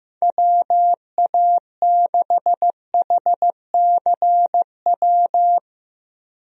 Morse code, heard as a keyed tone, about 15 wpm, 700 Hz. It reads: WA6HCW
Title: USA Call Signs - 6 Characters - Repeated 2x - 15wpm